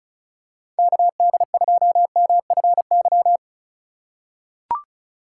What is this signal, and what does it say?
Morse code, heard as a keyed tone, about 36 wpm, 700 Hz. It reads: KD2MFY
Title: USA Call Signs - 6 Characters - 4x Character Spacing - 35wpm